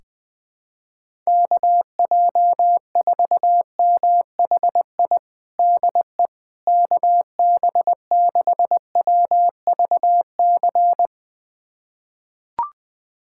Morse code, heard as a keyed tone, about 21 wpm, 700 Hz. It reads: KJ4MHI DE KB6WVC
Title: USA Callsign DE Callsign - 6 Characters - 20wpm